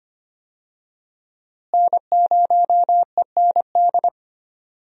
Morse code, heard as a keyed tone, about 25 wpm, 700 Hz. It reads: N0END
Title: USA Call Signs - 5 Characters - Repeated 3x - 25wpm